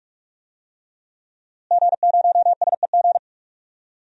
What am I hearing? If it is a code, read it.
Morse code, heard as a keyed tone, about 44 wpm, 700 Hz. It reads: G0SEG
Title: DXCC-100 Call Signs - 5 Characters - Repeated 3x - 45wpm